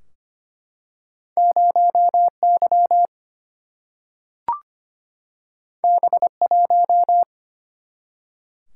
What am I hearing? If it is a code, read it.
Morse code, heard as a keyed tone, about 25 wpm, 700 Hz. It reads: 0 Y B 1
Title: Single-Letter Single-Number 25wpm